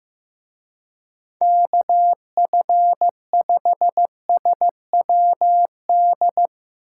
Morse code, heard as a keyed tone, about 15 wpm, 700 Hz. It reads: KF5SWD
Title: Speed Racing - USA 2x3 Call Signs 15wpm